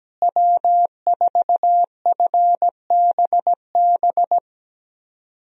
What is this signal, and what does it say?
Morse code, heard as a keyed tone, about 17 wpm, 700 Hz. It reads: W4FBB
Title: USA Call Signs - 5 Characters - Repeated 2x - 17wpm